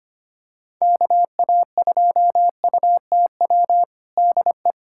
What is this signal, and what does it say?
Morse code, heard as a keyed tone, about 25 wpm, 700 Hz. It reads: KA2UTW DE
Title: USA Callsign DE Callsign 25wpm